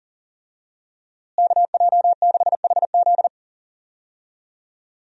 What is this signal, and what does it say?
Morse code, heard as a keyed tone, about 40 wpm, 700 Hz. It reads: KJ6HZ
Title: USA Call Signs - 5 Characters - Repeated 2x - 40wpm